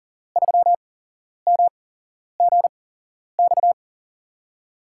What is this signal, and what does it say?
Morse code, heard as a keyed tone, about 41 wpm, spaced as for 11 wpm, 700 Hz. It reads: 3MGX
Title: USA Call Signs - 6 Characters - 8x Character Spacing - 40wpm